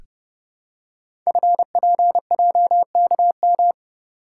Morse code, heard as a keyed tone, about 30 wpm, 700 Hz. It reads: FPJKM
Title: Code Groups - 5 Letter:Numbers - 30wpm